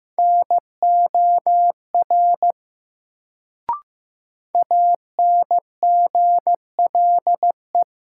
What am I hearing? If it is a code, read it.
Morse code, heard as a keyed tone, about 15 wpm, 700 Hz. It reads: NOR ANGLE
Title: Sentences from Top 1500 Words - New Word Drill - 15wpm